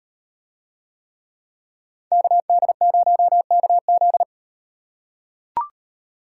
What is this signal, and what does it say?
Morse code, heard as a keyed tone, about 38 wpm, 700 Hz. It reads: KD0KZ